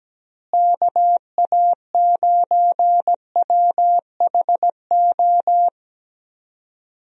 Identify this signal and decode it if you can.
Morse code, heard as a keyed tone, about 17 wpm, 700 Hz. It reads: KA9WHO
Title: USA Call Signs - 6 Characters - Repeated 2x - 17wpm